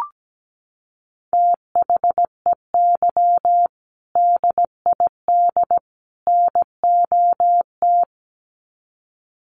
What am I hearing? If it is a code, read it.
Morse code, heard as a keyed tone, about 17 wpm, 700 Hz. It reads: THEY DID NOT
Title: Sets of 3 Words 17wpm